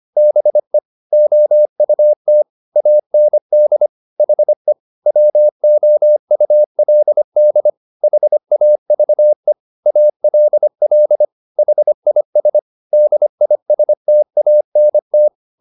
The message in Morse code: BE OUT AND HE WOULD HAVE ALL HIS DISTANT

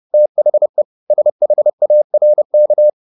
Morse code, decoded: THE SHARK

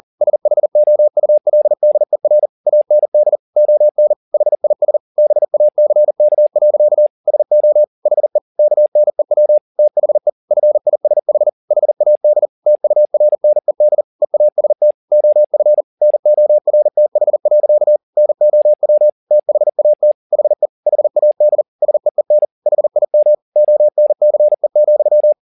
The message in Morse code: SHOULDER AND ON HIS BACK. SO HE KNEW THE FISH HAD TURNED EAST OF NORTH. NOW THAT HE HAD SEEN HIM ONCE,

